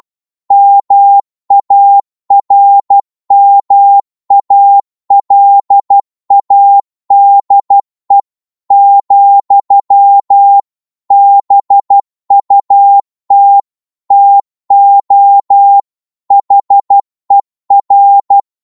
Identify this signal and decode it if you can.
Morse code, heard as a keyed tone, about 12 wpm, 800 Hz. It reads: MARMALADE, BUT TO HER
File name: cwmp3 / psg-1-12wpm